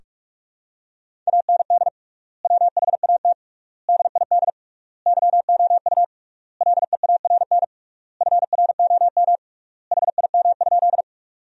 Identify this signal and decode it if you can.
Morse code, heard as a keyed tone, about 43 wpm, 700 Hz. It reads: AND WHAT DID YOU LEARN FROM HIM?
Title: 3x Word-Spacing - Sentences from Top 300 Words - 45wpm